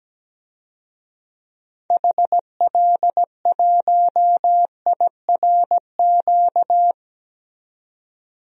Morse code, heard as a keyed tone, about 17 wpm, 700 Hz. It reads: HL1IRQ
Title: DXCC-100 Call Signs - 6 Characters - 17wpm